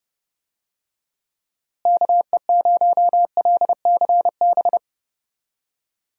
Morse code, heard as a keyed tone, about 30 wpm, 700 Hz. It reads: KE0LCB
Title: USA Call Signs - 6 Characters - Repeated 3x - 30wpm